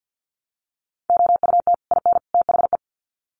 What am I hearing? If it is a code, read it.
Morse code, heard as a keyed tone, about 48 wpm, 700 Hz. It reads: OUT IN THE
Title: Sets of 3 Words 50wpm